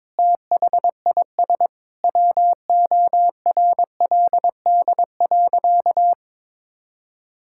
Morse code, heard as a keyed tone, about 22 wpm, 700 Hz. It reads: THIS WORLD.
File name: Sentences with 9-Word Phrases - 22wpm